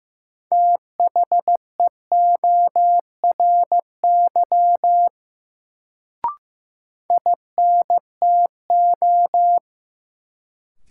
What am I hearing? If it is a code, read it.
Morse code, heard as a keyed tone, about 15 wpm, 700 Hz. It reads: THEORY INTO
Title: Sentences from Top 600 Words - New Word Drill - 15wpm